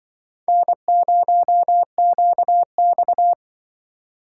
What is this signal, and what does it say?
Morse code, heard as a keyed tone, about 25 wpm, 700 Hz. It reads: N0QX